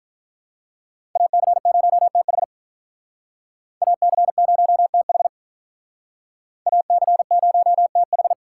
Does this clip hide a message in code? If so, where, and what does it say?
Yes: Morse code from 1.2 to 8.3 s, heard as a keyed tone, about 49 wpm, 700 Hz. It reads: AC0TH AC0TH AC0TH